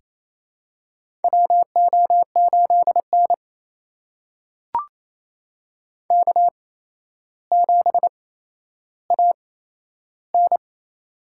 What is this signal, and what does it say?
Morse code, heard as a keyed tone, about 28 wpm, spaced as for 12 wpm, 700 Hz. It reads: WO8N K7AN
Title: USA Call Signs - 4 Characters - 8x Character Spacing - 28wpm